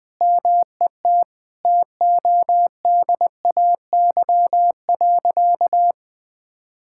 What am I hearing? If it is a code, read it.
Morse code, heard as a keyed tone, about 20 wpm, 700 Hz. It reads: MET TODAY.
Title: Sentences with 3-Word Phrases - 20wpm